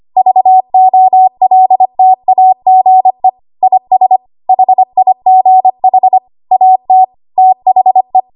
Morse code, heard as a keyed tone, about 25 wpm, 750 Hz. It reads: VOLTAGE IS HIGH AT THE
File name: Archive / 25 WPM / 25WPM